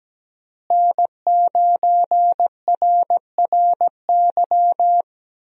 Morse code, heard as a keyed tone, about 17 wpm, 700 Hz. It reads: N9RRY
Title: USA Call Signs - 5 Characters - Repeated 2x - 17wpm